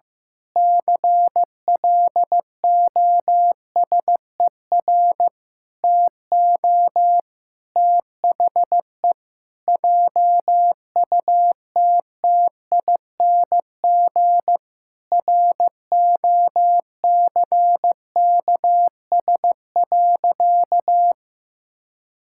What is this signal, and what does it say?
Morse code, heard as a keyed tone, about 15 wpm, 700 Hz. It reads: CLOSER TO THE JUTTING ROCKS.
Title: Pangram Challenge v2 - 15wpm